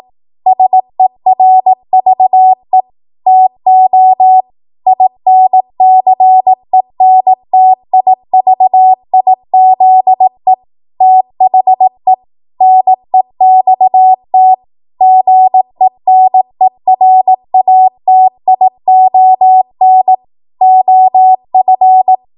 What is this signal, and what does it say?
Morse code, heard as a keyed tone, about 18 wpm, 750 Hz. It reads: SERVE TO INCENTIVIZE THE NEXT GENERATION OF
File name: Morse / 18WPM